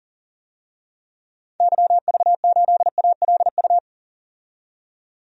Morse code, heard as a keyed tone, about 40 wpm, 700 Hz. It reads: YV8ALU
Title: DXCC-100 Call Signs - 6 Characters - 40wpm